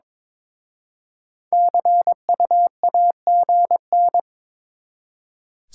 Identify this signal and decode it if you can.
Morse code, heard as a keyed tone, about 27 wpm, 700 Hz. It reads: CU AGN
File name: Speed-Racing Top 20 QSO Elements - 22wpm